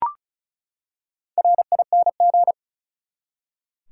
Morse code, heard as a keyed tone, about 36 wpm, 700 Hz. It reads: RING
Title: Four Letter Words - 35wpm